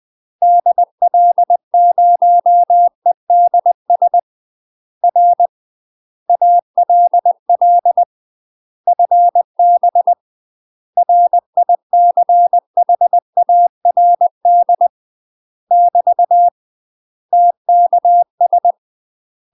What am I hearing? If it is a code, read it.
Morse code, heard as a keyed tone, about 20 wpm, 700 Hz. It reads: DL0EDS R ALL FB RICHARD <BT> TKS
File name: UK MASTER QSOs TRAINING / 20WPM / UK MASTER QSOs 0046